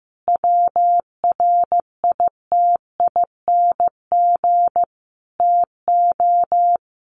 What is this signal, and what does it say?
Morse code, heard as a keyed tone, about 15 wpm, 700 Hz. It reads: WRITING TO
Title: Sentences from Top 800 Words 15wpm